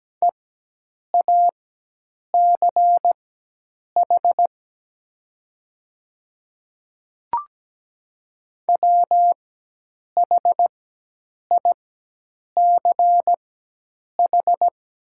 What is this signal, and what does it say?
Morse code, heard as a keyed tone, about 17 wpm, spaced as for 8 wpm, 700 Hz. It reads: EACH WHICH
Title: 4x Character Spacing - Top 100 Words - 17wpm